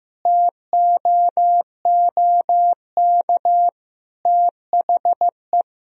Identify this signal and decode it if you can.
Morse code, heard as a keyed tone, about 15 wpm, 700 Hz. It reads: TOOK THE
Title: Sentences from Top 1000 Words - Review Part 2 - 15wpm